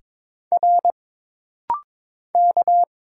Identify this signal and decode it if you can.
Morse code, heard as a keyed tone, about 22 wpm, 700 Hz. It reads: R K
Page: This rapid-fire timing is geared to push your ICR (Instant Character Recognition) skills.